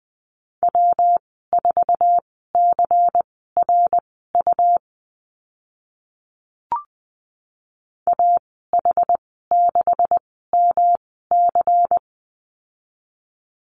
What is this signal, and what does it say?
Morse code, heard as a keyed tone, about 20 wpm, spaced as for 15 wpm, 700 Hz. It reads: W4CRU AH6MC
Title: 154 Lesson 25g - Letter p - Call Signs 20wpm